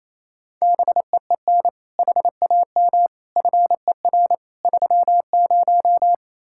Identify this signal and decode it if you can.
Morse code, heard as a keyed tone, about 37 wpm, 700 Hz. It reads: BEEN HAM FER 30